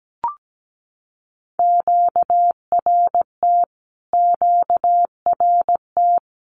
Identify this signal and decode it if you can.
Morse code, heard as a keyed tone, about 17 wpm, 700 Hz. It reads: QRT QRT
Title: Q-Codes Repeated 3x 17wpm